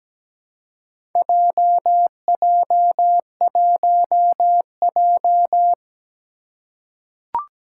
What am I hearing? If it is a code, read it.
Morse code, heard as a keyed tone, about 17 wpm, 700 Hz. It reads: JJ1J
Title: Four Challenging Letter-Number Combinations 17wpm